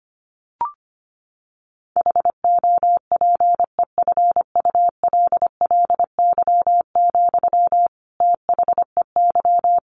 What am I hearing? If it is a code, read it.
Morse code, heard as a keyed tone, about 25 wpm, 700 Hz. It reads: HOPEFULLY, THEY
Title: Sentences from Top 1500 Words 25wpm